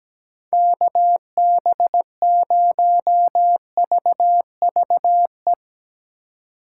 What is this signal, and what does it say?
Morse code, heard as a keyed tone, about 17 wpm, 700 Hz. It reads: KB0VVE
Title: USA Call Signs - 6 Characters - 3x Character Spacing - 17wpm